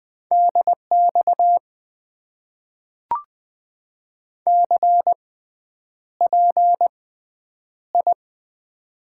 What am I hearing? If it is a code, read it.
Morse code, heard as a keyed tone, about 20 wpm, spaced as for 9 wpm, 700 Hz. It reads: DX CPI
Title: Top 90 QSO Elements - 6x Character Spacing - 20wpm